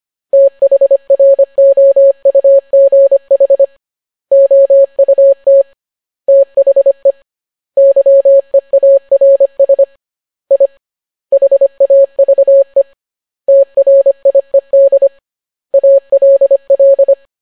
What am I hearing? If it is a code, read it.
Morse code, heard as a keyed tone, about 25 wpm, 550 Hz. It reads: THROUGH OUT THE YEARS I HAVE TRIED ALL